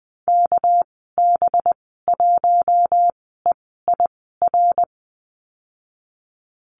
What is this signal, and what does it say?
Morse code, heard as a keyed tone, about 20 wpm, spaced as for 14 wpm, 700 Hz. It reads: KB1EIR
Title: 231 Lesson 37g - Letter x - Call Signs 20wpm